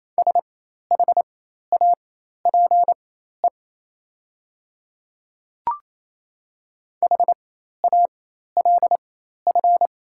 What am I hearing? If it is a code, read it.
Morse code, heard as a keyed tone, about 28 wpm, spaced as for 13 wpm, 700 Hz. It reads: SHAPE HALF
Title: 4x Character Spacing - Top 500 Words - 28wpm